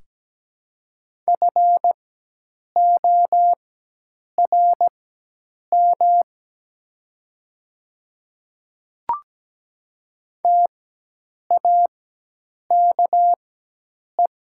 Morse code, heard as a keyed tone, about 17 wpm, spaced as for 8 wpm, 700 Hz. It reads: FORM TAKE
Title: 4x Character Spacing - Top 300 Words - 17wpm